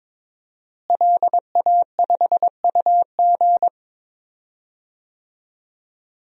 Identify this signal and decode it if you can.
Morse code, heard as a keyed tone, about 21 wpm, 700 Hz. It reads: LA5UG
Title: DXCC-100 Call Signs - 5 Characters - Repeated 2x - 22wpm